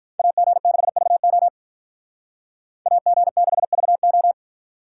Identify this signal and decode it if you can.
Morse code, heard as a keyed tone, about 46 wpm, 700 Hz. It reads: AG6VO AG6VO